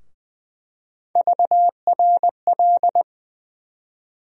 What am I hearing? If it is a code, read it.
Morse code, heard as a keyed tone, about 20 wpm, 700 Hz. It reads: VRL
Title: USA Call Sign Postfixes - 3 Letters 20wpm